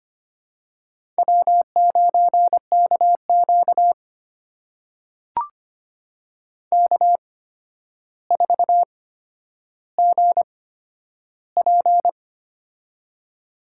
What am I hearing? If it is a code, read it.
Morse code, heard as a keyed tone, about 25 wpm, spaced as for 11 wpm, 700 Hz. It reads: W9KQ K4GP
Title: USA Call Signs - 4 Characters - 8x Character Spacing - 25wpm